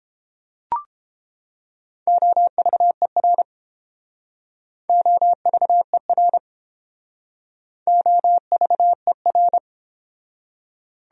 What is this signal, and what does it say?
Morse code, heard as a keyed tone, about 30 wpm, 700 Hz. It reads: OVER OVER OVER